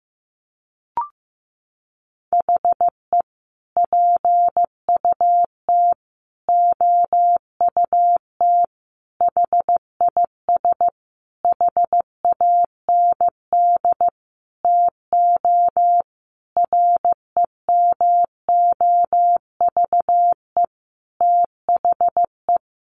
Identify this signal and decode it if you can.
Morse code, heard as a keyed tone, about 15 wpm, 700 Hz. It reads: HE PUT OUT HIS HAND TO REMOVE THE
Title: Sentences from Top 800 Words 15wpm